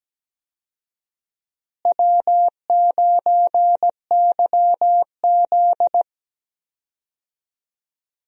Morse code, heard as a keyed tone, about 17 wpm, 700 Hz. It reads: W9YZ
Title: USA Call Signs - 4 Characters - Repeated 3x - 17wpm